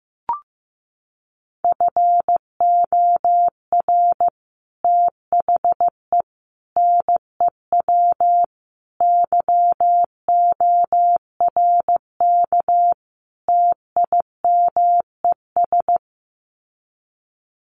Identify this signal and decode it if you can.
Morse code, heard as a keyed tone, about 15 wpm, 700 Hz. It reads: FOR THE NEW YORK TIMES
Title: Sets of 5 Words 15wpm